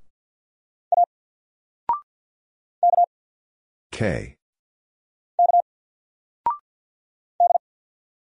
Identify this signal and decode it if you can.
Morse code, heard as a keyed tone, about 48 wpm, 700 Hz. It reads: A K K D
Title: Single Letter-Number 50wpm